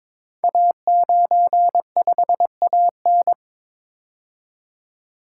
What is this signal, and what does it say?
Morse code, heard as a keyed tone, about 22 wpm, 700 Hz. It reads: A95AN
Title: Code Groups - 5 Letter:Numbers - 22wpm